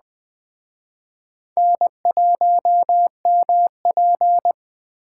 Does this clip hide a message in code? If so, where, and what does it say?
Yes: Morse code from 1.6 to 4.5 s, heard as a keyed tone, about 20 wpm, 700 Hz. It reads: N1MP